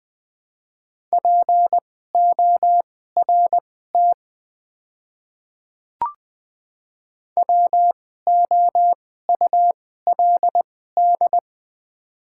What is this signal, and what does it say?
Morse code, heard as a keyed tone, about 21 wpm, spaced as for 15 wpm, 700 Hz. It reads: PORT WOULD
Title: 2x Character Spacing - Top 200 Words - 20wpm